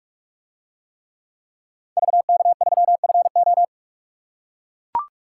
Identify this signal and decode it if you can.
Morse code, heard as a keyed tone, about 44 wpm, 700 Hz. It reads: VK3FO